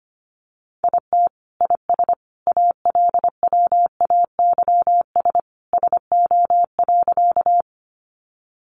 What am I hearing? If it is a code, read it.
Morse code, heard as a keyed tone, about 25 wpm, 700 Hz. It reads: IT IS ALWAYS SO.
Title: Sentences from Top 300 Words 25wpm